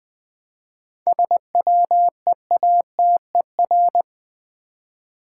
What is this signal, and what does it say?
Morse code, heard as a keyed tone, about 20 wpm, 700 Hz. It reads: SWEATER
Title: Difficult Words with Hints - 20wpm